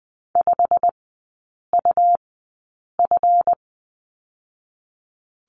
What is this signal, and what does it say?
Morse code, heard as a keyed tone, about 20 wpm, spaced as for 8 wpm, 700 Hz. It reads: 5UF
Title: 218 Lesson 35e - Number 0 - Three Characters 20wpm